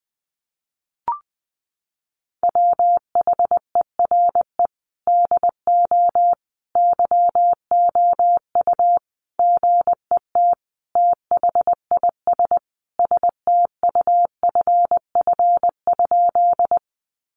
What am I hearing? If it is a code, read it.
Morse code, heard as a keyed tone, about 19 wpm, 700 Hz. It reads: WHERE DO YOU GET THIS STUFF?
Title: Sentences from Top 1400 Words 20wpm